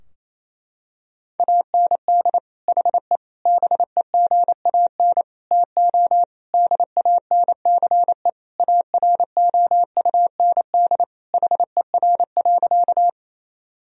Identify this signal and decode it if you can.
Morse code, heard as a keyed tone, about 28 wpm, 700 Hz. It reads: AND HE BEGAN TO DANCE AROUND HER.
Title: Sentences from Top 1000 Words - Review Part 4 - 28wpm